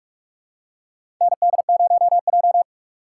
Morse code, heard as a keyed tone, about 46 wpm, 700 Hz. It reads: ND0J
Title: USA Call Signs - 4 Characters - 2x Character Spacing - 45wpm